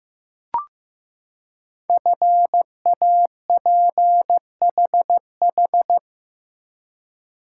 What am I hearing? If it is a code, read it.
Morse code, heard as a keyed tone, about 15 wpm, 700 Hz. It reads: FAPHH
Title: Code Groups - 5 Letters - 15wpm